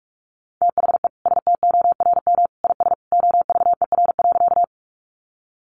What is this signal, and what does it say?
Morse code, heard as a keyed tone, about 44 wpm, 700 Hz. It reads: THE STORM IS OVER.
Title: Sentences from Top 1500 Words 45wpm